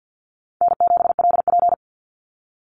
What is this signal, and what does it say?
Morse code, heard as a keyed tone, about 49 wpm, 700 Hz. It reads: N7LP
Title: NA SOTA Chaser Callsigns 50wpm